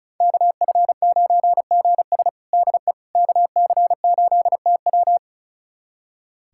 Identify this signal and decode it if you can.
Morse code, heard as a keyed tone, about 34 wpm, 700 Hz. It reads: KF9GS DE KC8TW
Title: USA Callsign DE Callsign - 5 Characters - 35wpm